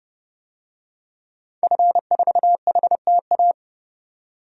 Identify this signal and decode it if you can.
Morse code, heard as a keyed tone, about 30 wpm, 700 Hz. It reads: F4HTA
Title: DXCC-100 Call Signs - 5 Characters - 30wpm